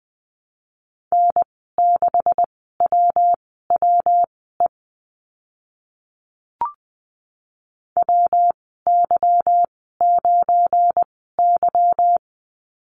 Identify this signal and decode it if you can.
Morse code, heard as a keyed tone, about 20 wpm, spaced as for 14 wpm, 700 Hz. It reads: N6WWE WY9Y
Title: 193 Lesson 31g - Letter v - Call Signs 20wpm